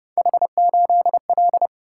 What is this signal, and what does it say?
Morse code, heard as a keyed tone, about 31 wpm, 700 Hz. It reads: H8L